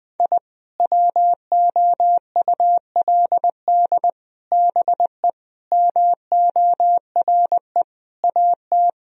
Morse code, heard as a keyed tone, about 21 wpm, 700 Hz. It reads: I WOULD BE MORE AT
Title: Sentences from Top 1000 Words - Review Part 1 - 20wpm